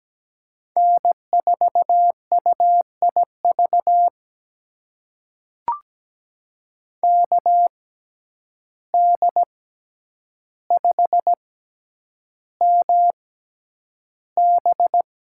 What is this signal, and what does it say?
Morse code, heard as a keyed tone, about 17 wpm, spaced as for 9 wpm, 700 Hz. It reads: N4UIV KD5MB
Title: USA Call Signs - 5 Characters - 6x Character Spacing - 17wpm